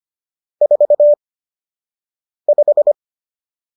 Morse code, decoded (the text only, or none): 45